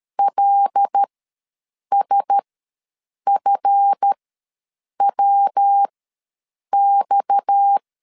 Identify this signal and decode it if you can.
Morse code, heard as a keyed tone, about 13 wpm, spaced as for 7 wpm, 800 Hz. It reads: LSFWX
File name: LEZIONI / 7 GF9Z / 7 GF ws 12wpm 3140